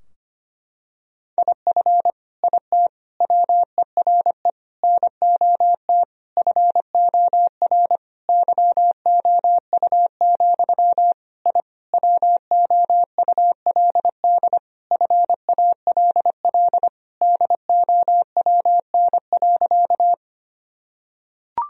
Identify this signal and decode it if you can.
Morse code, heard as a keyed tone, about 25 wpm, 700 Hz. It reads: IF IT WERE NOT FOR YOU, I WOULD FALL DOWN.
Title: Sentences from Top 1000 Words - Review Part 2 - 25wpm